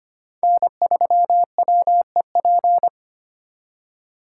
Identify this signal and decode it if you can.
Morse code, heard as a keyed tone, about 31 wpm, 700 Hz. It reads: N3WEP
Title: Speed Racing - USA Call Signs 25wpm